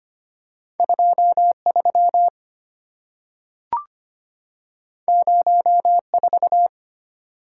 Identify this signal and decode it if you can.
Morse code, heard as a keyed tone, about 25 wpm, 700 Hz. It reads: 23 04
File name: Two Letter-Number Combinations - 25wpm